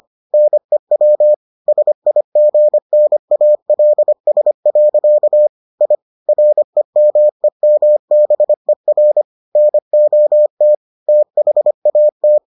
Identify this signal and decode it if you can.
Morse code, heard as a keyed tone, about 25 wpm, 600 Hz. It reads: NEW SIGNALS. I REMEMBER NOT THAT